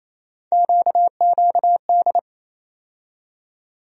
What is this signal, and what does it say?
Morse code, heard as a keyed tone, about 28 wpm, 700 Hz. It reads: QQD
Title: USA Call Sign Postfixes - 3 Letters 28wpm